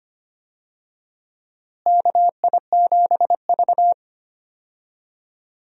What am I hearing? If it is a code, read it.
Morse code, heard as a keyed tone, about 25 wpm, 700 Hz. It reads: KI7V
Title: USA Call Signs - 4 Characters - Repeated 2x - 25wpm